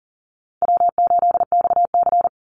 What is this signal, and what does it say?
Morse code, heard as a keyed tone, about 40 wpm, 700 Hz. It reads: W8XC
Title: Easy Call Signs Repeated 2x - 40wpm